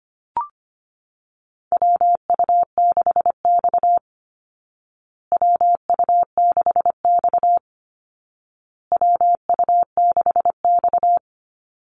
Morse code, heard as a keyed tone, about 25 wpm, 700 Hz. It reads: WU6X WU6X WU6X
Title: Call Signs Repeated 2x - 25wpm